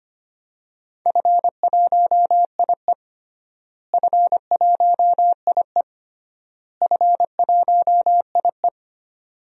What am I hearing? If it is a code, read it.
Morse code, heard as a keyed tone, about 25 wpm, 700 Hz. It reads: F1IE F1IE F1IE
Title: DXCC-100 Call Signs - 4 Characters - Repeated 3x - 25wpm